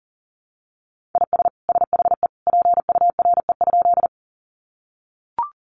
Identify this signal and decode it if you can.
Morse code, heard as a keyed tone, about 40 wpm, 700 Hz. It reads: IS SHE PURE?
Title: Sentences from Top 1500 Words 40wpm